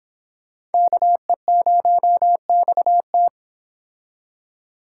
Morse code, heard as a keyed tone, about 30 wpm, 700 Hz. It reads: KE0XT